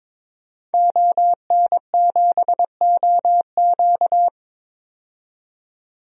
Morse code, heard as a keyed tone, about 22 wpm, 700 Hz. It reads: ON7OQ
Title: DXCC-100 Call Signs - 5 Characters - 22wpm